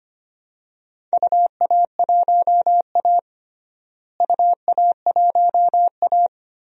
Morse code, heard as a keyed tone, about 25 wpm, 700 Hz. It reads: UA1A UA1A
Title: DXCC-100 Call Signs - 4 Characters - Repeated 2x - 25wpm